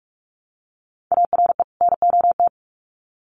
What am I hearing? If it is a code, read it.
Morse code, heard as a keyed tone, about 46 wpm, 700 Hz. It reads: ARE NOT
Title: Sets of 2 Words 45wpm